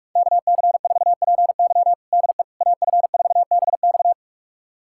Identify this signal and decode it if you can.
Morse code, heard as a keyed tone, about 46 wpm, 700 Hz. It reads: KC4PY DE AF4BX
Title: USA Callsign DE Callsign - 5 Characters - 45wpm